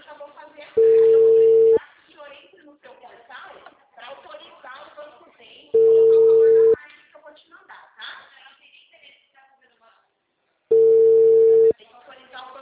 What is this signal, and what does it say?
Ringback tone (China, 450 Hz), with speech.